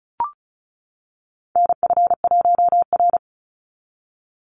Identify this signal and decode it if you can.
Morse code, heard as a keyed tone, about 34 wpm, 700 Hz. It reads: NF1R
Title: NA SOTA Chaser Callsigns 35wpm